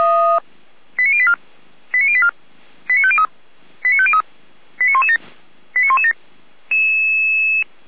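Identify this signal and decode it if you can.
Five-tone selective call (ZVEI), heard as quick runs of five tones, tones of about 70 ms. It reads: 89084 89084 89503 89503 89117 89117
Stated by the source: ZVEI five-tone call 89120 received by SDR.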